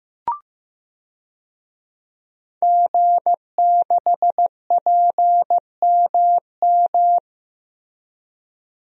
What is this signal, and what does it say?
Morse code, heard as a keyed tone, about 15 wpm, 700 Hz. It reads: G6PMM
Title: DXCC-100 Call Signs - 5 Characters - Repeated 2x - 15wpm